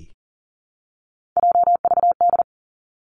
Morse code, heard as a keyed tone, about 38 wpm, 700 Hz. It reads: JVD
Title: Three Challenging Letter-Number Combinations 40wpm